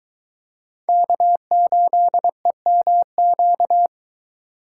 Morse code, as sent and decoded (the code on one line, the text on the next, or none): -.- ---.. -- --.-
K8EMQ